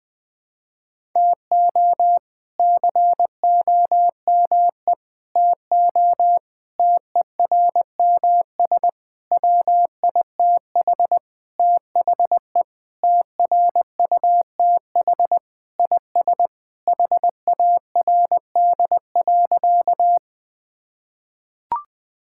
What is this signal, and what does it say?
Morse code, heard as a keyed tone, about 21 wpm, 700 Hz. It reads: TO COME TO TERMS WITH THE TRUTH IS HARD.
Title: Sentences with 6-Word Phrases - Spoken Beginning - 20wpm